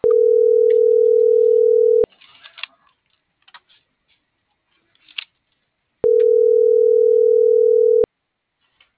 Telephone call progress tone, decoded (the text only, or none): ringback tone